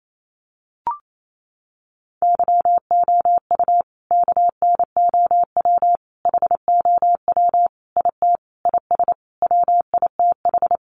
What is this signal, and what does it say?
Morse code, heard as a keyed tone, about 28 wpm, 700 Hz. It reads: YOU KNOW HOW IT IS WITH
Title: Sentences from Top 100 Words 28wpm